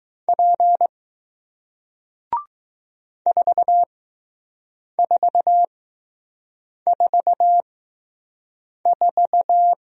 Morse code, heard as a keyed tone, about 20 wpm, spaced as for 6 wpm, 700 Hz. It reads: P 4444